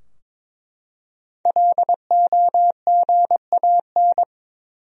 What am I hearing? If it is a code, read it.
Morse code, heard as a keyed tone, about 21 wpm, 700 Hz. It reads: LOGAN
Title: USA Common Names - 22wpm